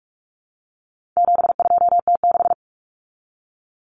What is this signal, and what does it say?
Morse code, heard as a keyed tone, about 44 wpm, 700 Hz. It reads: 72T6
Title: Common Cut Numbers - 4-Digit 45wpm